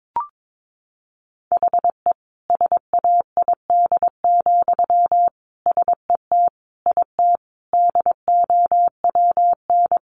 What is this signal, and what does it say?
Morse code, heard as a keyed tone, about 22 wpm, 700 Hz. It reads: HE SAID, SET IT DOWN
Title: Sentences from Top 1400 Words 22wpm